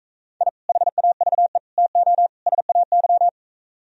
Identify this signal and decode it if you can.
Morse code, heard as a keyed tone, about 45 wpm, 700 Hz. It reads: I HAVE TO SAY